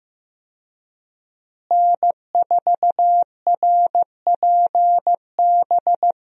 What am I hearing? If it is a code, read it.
Morse code, heard as a keyed tone, about 15 wpm, 700 Hz. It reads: N4RPB
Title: USA Call Signs - 5 Characters - Repeated 3x - 15wpm